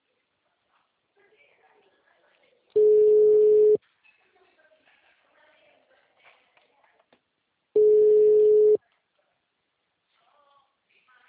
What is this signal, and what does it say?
Ringback tone (Europe, 425 Hz), with speech.